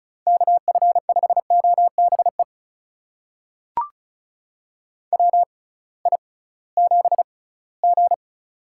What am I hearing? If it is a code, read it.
Morse code, heard as a keyed tone, about 36 wpm, spaced as for 20 wpm, 700 Hz. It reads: KF5OBE WI7G
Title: USA Call Signs - 6x Character Spacing - 35wpm